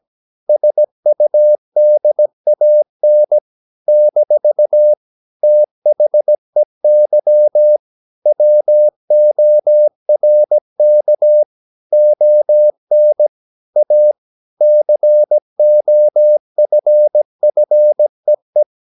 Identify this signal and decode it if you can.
Morse code, heard as a keyed tone, about 17 wpm, 600 Hz. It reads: SUDAN - THEY WORK ON A COFFEE
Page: Romanen Never af Ken Follet på engelsk.